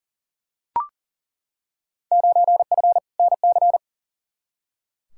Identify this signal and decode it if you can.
Morse code, heard as a keyed tone, about 40 wpm, 700 Hz. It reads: 9F NC
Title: Field Day Exchanges Encore - 40wpm